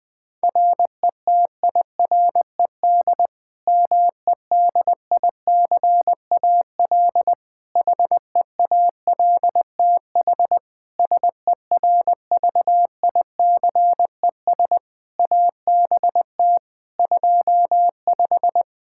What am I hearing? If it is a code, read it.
Morse code, heard as a keyed tone, about 20 wpm, 700 Hz. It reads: RETIRED MEDICAL HEALTH SERVICES ABT 25
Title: Practice QSO - Exchange 3 - More Info - 20wpm